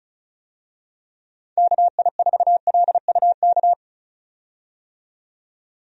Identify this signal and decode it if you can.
Morse code, heard as a keyed tone, about 34 wpm, 700 Hz. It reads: KI4LUK
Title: USA Call Signs - 6 Characters - 35wpm